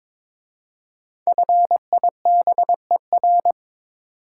Morse code, heard as a keyed tone, about 22 wpm, 700 Hz. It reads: FIBER
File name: 5 Letter Words - First 3 Letters Given - 22wpm